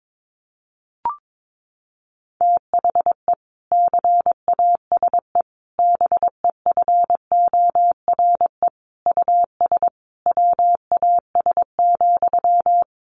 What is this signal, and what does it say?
Morse code, heard as a keyed tone, about 22 wpm, 700 Hz. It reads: THE CASE BEFORE US WAS,
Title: Sentences from Top 1400 Words 22wpm